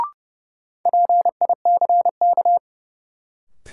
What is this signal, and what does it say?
Morse code, heard as a keyed tone, about 31 wpm, 700 Hz. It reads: PICK
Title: Sentences from Top 1000 Words - New Word Drill - 30wpm